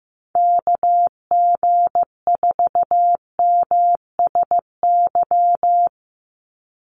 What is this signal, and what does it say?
Morse code, heard as a keyed tone, about 15 wpm, 700 Hz. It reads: KG4MSY
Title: USA Call Signs 15wpm